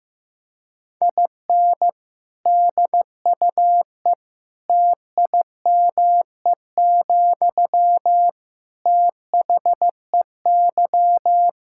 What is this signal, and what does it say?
Morse code, heard as a keyed tone, about 15 wpm, 700 Hz. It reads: IN DUE TIME, THEY
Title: Sentences from Top 1000 Words - Review Part 6 - 15wpm